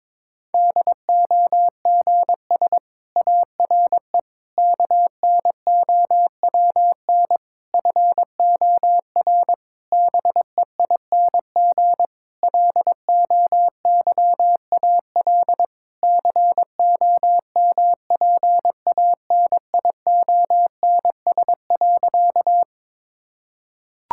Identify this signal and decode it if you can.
Morse code, heard as a keyed tone, about 22 wpm, 700 Hz. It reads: DOGS ARE KNOWN FOR BEING LOYAL COMPANIONS.
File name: Difficult Sentences with Hints - 22wpm